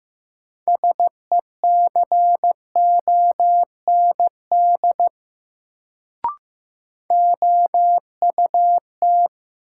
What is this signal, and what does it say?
Morse code, heard as a keyed tone, about 15 wpm, 700 Hz. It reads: SECOND OUT
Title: Sets of 2 Words - New Word Drill - 15wpm